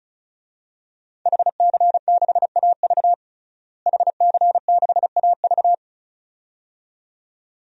Morse code, heard as a keyed tone, about 36 wpm, 700 Hz. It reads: HC6AV HC6AV
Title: DXCC-100 Call Signs - 5 Characters - Repeated 2x - 35wpm